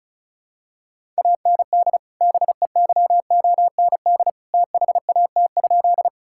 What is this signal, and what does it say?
Morse code, heard as a keyed tone, about 36 wpm, 700 Hz. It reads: AND BEYOND THAT?
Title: Sentences from Top 1000 Words - Review Part 6 - 35wpm